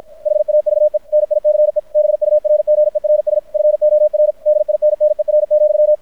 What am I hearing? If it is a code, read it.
Morse code, heard as a keyed tone, about 54 wpm, 600 Hz. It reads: HAVE NICE BURGERS FOR DINNER,
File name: archived / CRSnet